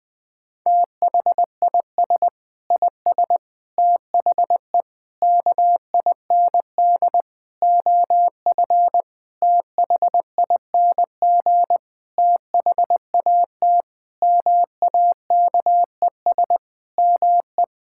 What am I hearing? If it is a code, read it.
Morse code, heard as a keyed tone, about 20 wpm, 700 Hz. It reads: THIS IS THE KIND OF THING THAT MAKES ME
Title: Sentences with 7-Word Phrases - 20wpm